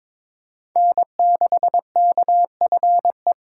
Morse code, heard as a keyed tone, about 22 wpm, 700 Hz. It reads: N6KFE
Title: USA Call Signs - 6x Character Spacing - 22wpm